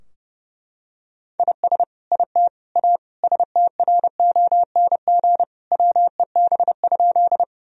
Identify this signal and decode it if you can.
Morse code, heard as a keyed tone, about 30 wpm, 700 Hz. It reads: IS IT A STRONG WEB?
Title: Sentences from Top 1000 Words - Review Part 4 - 30wpm